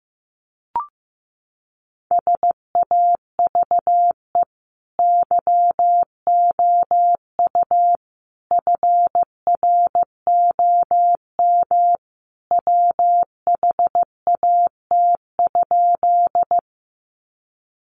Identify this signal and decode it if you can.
Morse code, heard as a keyed tone, about 15 wpm, 700 Hz. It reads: SAVE YOU FROM WHAT?
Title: Sentences from Top 1100 Words 15wpm